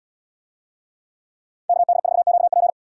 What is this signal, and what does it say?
Morse code, heard as a keyed tone, about 77 wpm, 700 Hz. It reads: KD4QP